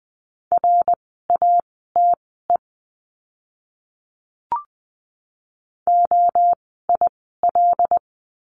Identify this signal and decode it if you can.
Morse code, heard as a keyed tone, about 20 wpm, spaced as for 14 wpm, 700 Hz. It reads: RATE OIL
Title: 101 Lesson 17f - Letter u - Words 20wpm